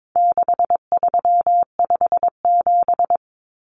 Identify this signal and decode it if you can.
Morse code, heard as a keyed tone, about 22 wpm, 700 Hz. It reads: 6357
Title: Common Cut Numbers - 4-Digit 22wpm